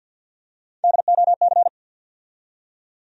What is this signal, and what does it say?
Morse code, heard as a keyed tone, about 53 wpm, 700 Hz. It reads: DOC
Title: USA Call Sign Postfixes - 3 Letters 50wpm